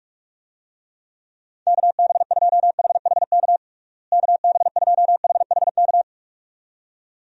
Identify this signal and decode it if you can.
Morse code, heard as a keyed tone, about 47 wpm, 700 Hz. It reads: KB2HHK KB2HHK
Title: USA Call Signs - 6 Characters - Repeated 2x - 45wpm